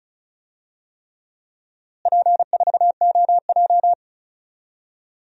Morse code, heard as a keyed tone, about 36 wpm, 700 Hz. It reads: P4OJ
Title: DXCC-100 Call Signs - 4 Characters - Repeated 3x - 35wpm